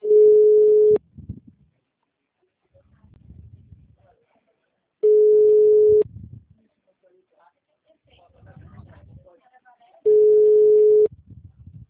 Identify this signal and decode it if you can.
Ringback tone (Europe, 425 Hz), with speech.